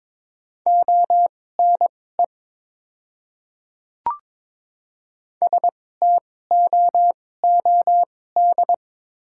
Morse code, heard as a keyed tone, about 22 wpm, spaced as for 16 wpm, 690 Hz. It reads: ONE STOOD
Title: 2x Character Spacing - Top 500 Words - 22wpm